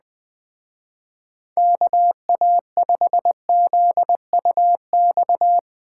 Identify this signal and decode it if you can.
Morse code, heard as a keyed tone, about 20 wpm, 700 Hz. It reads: KA5ZUX
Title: Speed Racing - USA 2x3 Call Signs 17wpm